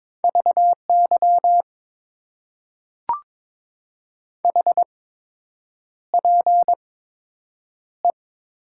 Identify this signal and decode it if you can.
Morse code, heard as a keyed tone, about 22 wpm, spaced as for 8 wpm, 700 Hz. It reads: VY HPE
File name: Top 90 QSO Elements - 8x Character Spacing - 22wpm